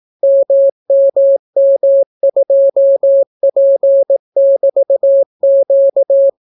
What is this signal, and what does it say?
Morse code, heard as a keyed tone, about 18 wpm, 550 Hz. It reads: MMM2P<BT>Q